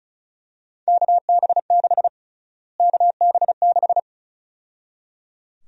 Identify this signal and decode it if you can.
Morse code, heard as a keyed tone, about 35 wpm, 700 Hz. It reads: KB6 KB6
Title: USA Call Sign Prefixes - Repeated 2x - 35wpm